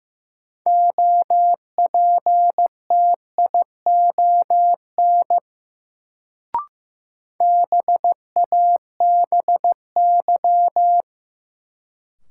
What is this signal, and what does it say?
Morse code, heard as a keyed tone, about 15 wpm, 700 Hz. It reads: OPTION BABY
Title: Sentences from Top 1300 Words - New Word Drill Alt - 15wpm